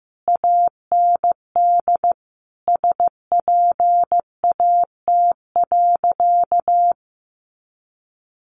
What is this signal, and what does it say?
Morse code, heard as a keyed tone, about 15 wpm, 700 Hz. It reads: AND SPAT.
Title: A Sound Of Thunder 15wpm